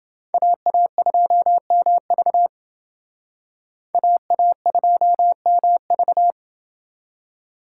Morse code, heard as a keyed tone, about 28 wpm, 700 Hz. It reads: AA2MV AA2MV